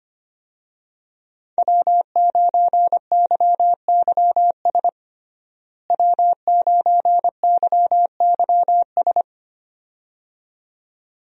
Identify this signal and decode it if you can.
Morse code, heard as a keyed tone, about 25 wpm, 700 Hz. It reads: W9YYS W9YYS
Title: USA Call Signs - 5 Characters - Repeated 2x - 25wpm